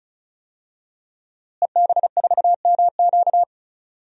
Morse code, heard as a keyed tone, about 36 wpm, 700 Hz. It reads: EB4MQ
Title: DXCC-100 Call Signs - 5 Characters - Repeated 3x - 35wpm